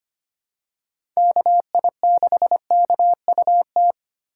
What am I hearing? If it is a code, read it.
Morse code, heard as a keyed tone, about 25 wpm, 700 Hz. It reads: KI6KUT
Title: USA Call Signs - 6 Characters - 2x Character Spacing - 25wpm